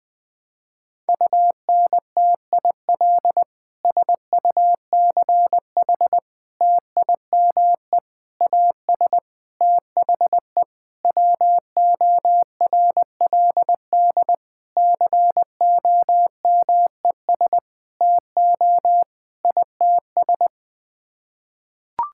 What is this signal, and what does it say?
Morse code, heard as a keyed tone, about 21 wpm, 700 Hz. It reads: UNTIL SUCH TIME AS THE WORLD COMES TO ITS
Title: Sets of 9 Words - 20wpm